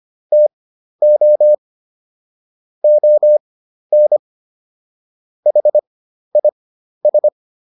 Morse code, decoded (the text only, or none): TO ON HIS